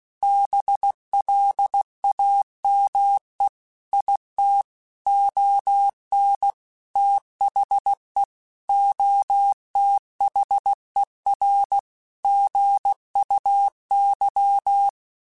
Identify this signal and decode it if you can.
Morse code, heard as a keyed tone, about 16 wpm, 800 Hz. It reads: BLAME IT ON THE OTHER GUY